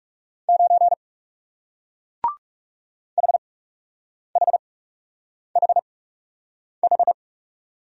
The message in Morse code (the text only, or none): 9 HHHH